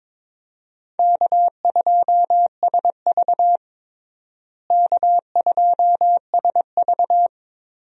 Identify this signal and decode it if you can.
Morse code, heard as a keyed tone, about 22 wpm, 700 Hz. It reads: K2SV K2SV
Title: USA Call Signs - 4 Characters - Repeated 3x - 22wpm